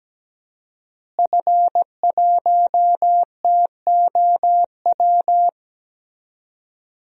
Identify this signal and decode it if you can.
Morse code, heard as a keyed tone, about 17 wpm, 700 Hz. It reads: F1TOW
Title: DXCC-100 Call Signs - 5 Characters - Repeated 2x - 17wpm